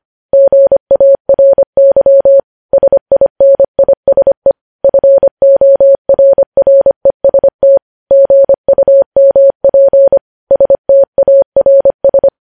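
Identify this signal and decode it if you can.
Morse code, heard as a keyed tone, about 25 wpm, 550 Hz. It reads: GARY SINISE FORREST GUMP STARS
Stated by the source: Morse code news headlines sent at 25 wpm.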